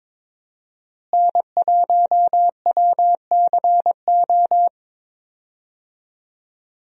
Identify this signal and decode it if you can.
Morse code, heard as a keyed tone, about 22 wpm, 700 Hz. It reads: N1WCO
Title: USA Call Signs - 5 Characters - Repeated 3x - 22wpm